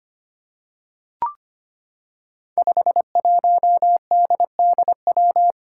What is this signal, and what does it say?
Morse code, heard as a keyed tone, about 25 wpm, 700 Hz. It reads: 51DDW
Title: Code Groups - 5 Letter:Numbers - 25wpm